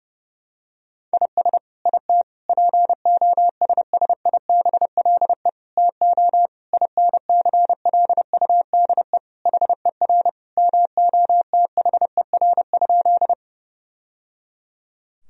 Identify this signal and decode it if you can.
Morse code, heard as a keyed tone, about 32 wpm, 700 Hz. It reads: IS IT POSSIBLE TO INCLUDE HER MOTHER?
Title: Sentences from Top 1000 Words - Review Part 3 - 30wpm